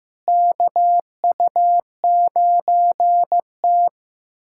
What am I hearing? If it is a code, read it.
Morse code, heard as a keyed tone, about 15 wpm, 700 Hz. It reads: KU9T
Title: USA Callsign DE Callsign - 4 Characters - 15wpm